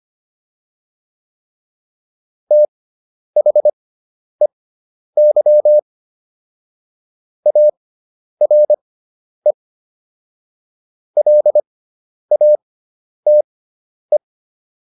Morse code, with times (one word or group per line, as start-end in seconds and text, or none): 2.5-5.8 THEY
7.5-9.5 ARE
11.2-14.2 LATE